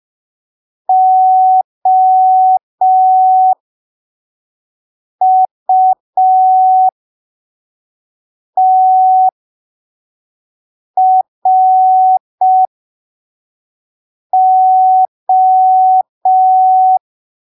Morse code, heard as a keyed tone, about 5 wpm, spaced as for 3 wpm, 750 Hz. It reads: OUTRO
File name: arquivos / texto / texto 19 05ppm